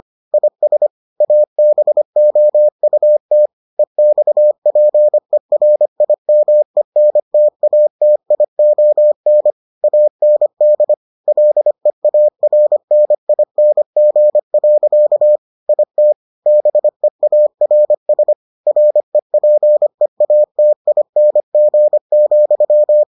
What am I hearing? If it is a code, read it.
Morse code, heard as a keyed tone, about 25 wpm, 600 Hz. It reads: IS ABOUT EXPERIMENTATION AND LEARNING. IT BEARS REPEATING,